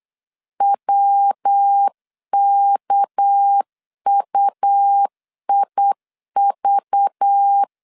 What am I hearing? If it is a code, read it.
Morse code, heard as a keyed tone, about 8 wpm, 800 Hz. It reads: WKUIV
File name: LEZIONI / 9 28 8wpm 3512